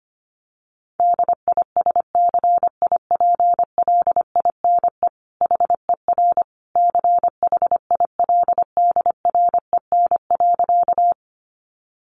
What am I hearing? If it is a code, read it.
Morse code, heard as a keyed tone, about 25 wpm, 700 Hz. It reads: DISCIPLINE HER CHILDREN.
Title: Sentences from Top 900 Words 25wpm